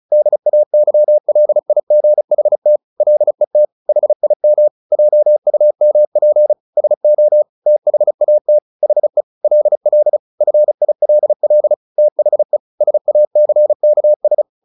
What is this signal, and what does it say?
Morse code, heard as a keyed tone, about 34 wpm, 600 Hz. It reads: DAYLIGHT LET HIM JUMP SO THAT HE LL FILL THE SACKS
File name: the old man and the sea / the old man and the sea 1 35wpm